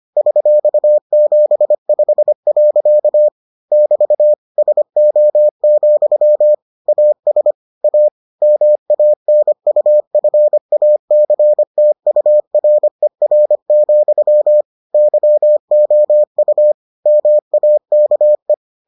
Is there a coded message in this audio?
Yes: Morse code from 0.2 to 18.5 s, heard as a keyed tone, about 25 wpm, 600 Hz. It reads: $75. <BT> SO, AS A MANUFACTURER, YOU MAKE